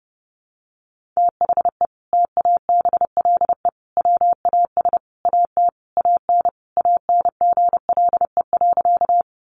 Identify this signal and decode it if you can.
Morse code, heard as a keyed tone, about 30 wpm, 700 Hz. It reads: THE TABLE WAS AT AN ANGLE.
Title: Sentences from Top 1500 Words 30wpm